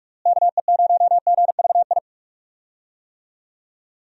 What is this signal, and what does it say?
Morse code, heard as a keyed tone, about 41 wpm, 700 Hz. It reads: KE0GVI
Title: USA Call Signs - 6 Characters - Repeated 2x - 45wpm